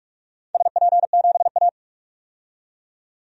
Morse code, heard as a keyed tone, about 43 wpm, 700 Hz. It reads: SP7A
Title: DXCC-100 Call Signs - 4 Characters - 45wpm